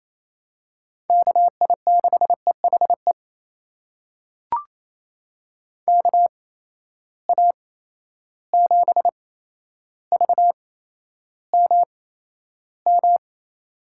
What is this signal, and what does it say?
Morse code, heard as a keyed tone, about 28 wpm, spaced as for 12 wpm, 700 Hz. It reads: KI6EHE KA7VMM
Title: USA Call Signs - 6 Characters - 8x Character Spacing - 28wpm